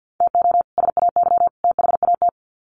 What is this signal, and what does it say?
Morse code, heard as a keyed tone, about 50 wpm, 700 Hz. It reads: TO SAY THAT
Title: Sets of 3 Words 50wpm